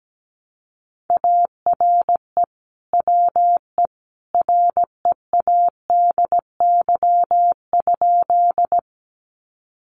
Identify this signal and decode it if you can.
Morse code, heard as a keyed tone, about 17 wpm, 700 Hz. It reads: ARE WE READY?
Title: Sentences from Top 400 Words 17wpm